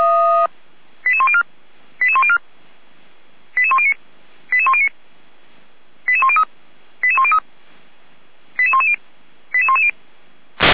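Five-tone selective call (ZVEI), heard as quick runs of five tones, tones of about 70 ms. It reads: 88104 88104 88109 88109 88103 88103 88110 88110
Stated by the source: ZVEI five-tone selective call received on an SDR.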